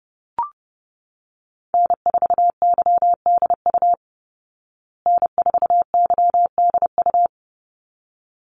Morse code, heard as a keyed tone, about 30 wpm, 700 Hz. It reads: N4YDU N4YDU
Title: Call Signs Repeated 3x - 30wpm